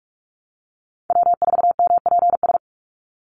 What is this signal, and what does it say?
Morse code, heard as a keyed tone, about 46 wpm, 690 Hz. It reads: W4MPS
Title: NA SOTA Chaser Callsigns 45wpm